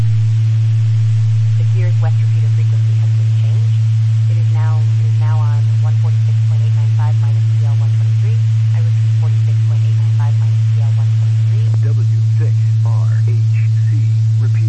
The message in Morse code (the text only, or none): none